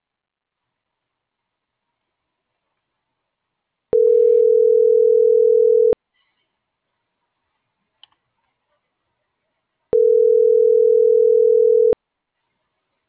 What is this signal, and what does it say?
Ringback tone (North America), 440 + 480 Hz.